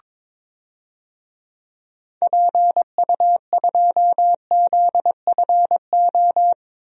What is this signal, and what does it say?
Morse code, heard as a keyed tone, about 22 wpm, 700 Hz. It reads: PU2ZFO
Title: DXCC-100 Call Signs - 6 Characters - Repeated 2x - 22wpm